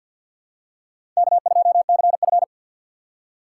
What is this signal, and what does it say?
Morse code, heard as a keyed tone, about 47 wpm, 700 Hz. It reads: K2CF
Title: USA Call Signs - 4 Characters - 2x Character Spacing - 50wpm